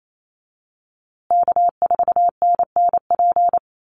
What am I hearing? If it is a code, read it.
Morse code, heard as a keyed tone, about 28 wpm, 690 Hz. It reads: K4NNP
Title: USA Call Signs Repeated 3x 28wpm